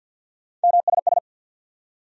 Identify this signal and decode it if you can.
Morse code, heard as a keyed tone, about 52 wpm, 700 Hz. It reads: MSS
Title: USA Call Sign Postfixes - 3 Letters 50wpm